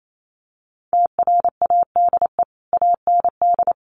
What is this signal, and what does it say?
Morse code, heard as a keyed tone, about 28 wpm, 700 Hz. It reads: TRADE AND
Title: Sentences from Top 1500 Words 28wpm